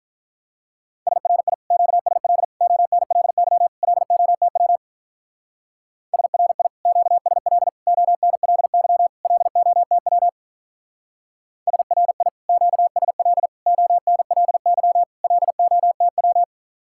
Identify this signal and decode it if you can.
Morse code, heard as a keyed tone, about 43 wpm, 700 Hz. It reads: SRI QSL ONLY LOTW SRI QSL ONLY LOTW SRI QSL ONLY LOTW
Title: Speed-Racing Common QSO Phrases - 35wpm